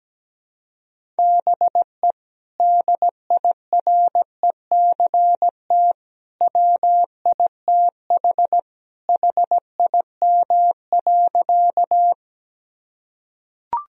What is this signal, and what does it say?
Morse code, heard as a keyed tone, about 17 wpm, 700 Hz. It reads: BE DIRECT WITH HIM.
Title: Sentences from Top 1000 Words - Review Part 2 - 17wpm